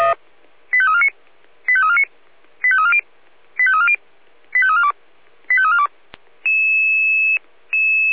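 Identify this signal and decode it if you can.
Five-tone selective call (ZVEI), heard as quick runs of five tones, tones of about 70 ms. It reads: 85349 85349 85340 85340 85342 85342
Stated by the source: ZVEI five-tone selective call received on an SDR.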